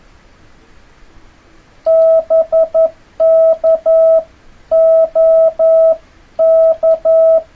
Morse code, heard as a keyed tone, about 11 wpm, 650 Hz. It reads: BK OK
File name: jh8cbh / 2026cw